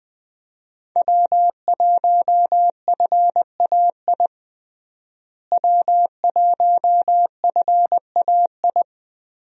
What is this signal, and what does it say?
Morse code, heard as a keyed tone, about 20 wpm, 700 Hz. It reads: W1FAI W1FAI
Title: USA Call Signs - 5 Characters - Repeated 2x - 20wpm